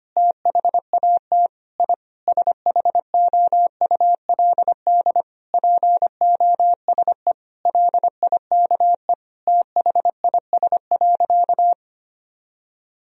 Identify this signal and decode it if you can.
Morse code, heard as a keyed tone, about 25 wpm, 700 Hz. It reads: THAT I SHOULD POSE LIKE THIS.
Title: Sentences from Top 1000 Words - Review Part 2 - 25wpm